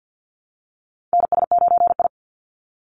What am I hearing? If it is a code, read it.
Morse code, heard as a keyed tone, about 48 wpm, 700 Hz. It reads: NS9I
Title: Call Signs Repeated 2x - 50wpm